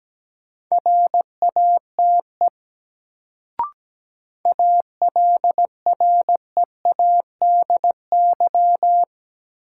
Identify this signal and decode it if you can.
Morse code, heard as a keyed tone, about 17 wpm, 700 Hz. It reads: RATE ALREADY
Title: Sentences from Top 700 Words - New Word Drill - 17wpm